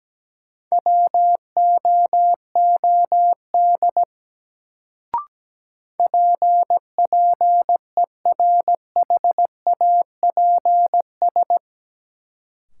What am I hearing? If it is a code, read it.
Morse code, heard as a keyed tone, about 17 wpm, 700 Hz. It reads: WOOD PPERHAPS
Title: Sentences from Top 1100 Words - New Word Drill Alt - 17wpm